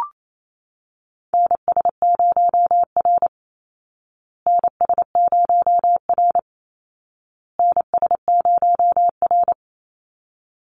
Morse code, heard as a keyed tone, about 28 wpm, 700 Hz. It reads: NS0R NS0R NS0R
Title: Call Signs Repeated 2x - 28wpm